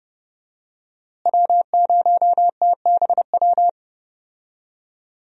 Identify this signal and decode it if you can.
Morse code, heard as a keyed tone, about 30 wpm, 700 Hz. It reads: W0TBW
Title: USA Call Signs - 5 Characters - Repeated 3x - 30wpm